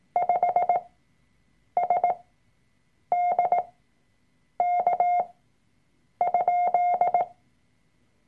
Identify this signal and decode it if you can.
Morse code, heard as a keyed tone, about 18 wpm, spaced as for 7 wpm, 710 Hz. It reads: 5SDK?
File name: Workshop Dateien / CW-Online-Kurs / Lektion 003